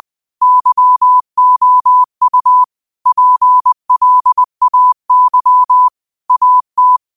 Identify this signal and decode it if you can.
Morse code, heard as a keyed tone, about 20 wpm, 1000 Hz. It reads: YOU PLAY AT
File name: qotdfiles / 20wpm